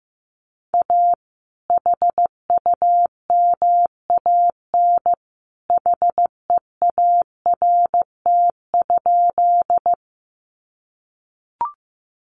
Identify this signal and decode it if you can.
Morse code, heard as a keyed tone, about 15 wpm, 700 Hz. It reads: A HUMAN HEART?
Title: Sentences from Top 1200 Words 15wpm